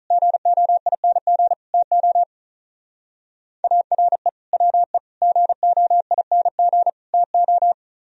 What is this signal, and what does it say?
Morse code, heard as a keyed tone, about 46 wpm, 700 Hz. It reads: GOING TO ARE WE GOING TO